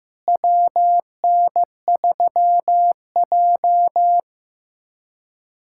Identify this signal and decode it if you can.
Morse code, heard as a keyed tone, about 16 wpm, 700 Hz. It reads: WN3J
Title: USA Callsign DE Callsign - 4 Characters - 15wpm